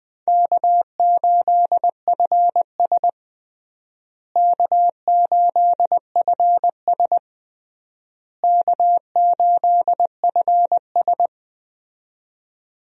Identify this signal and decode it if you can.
Morse code, heard as a keyed tone, about 20 wpm, 700 Hz. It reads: K8FS K8FS K8FS
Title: USA Call Signs - 4 Characters - Repeated 3x - 20wpm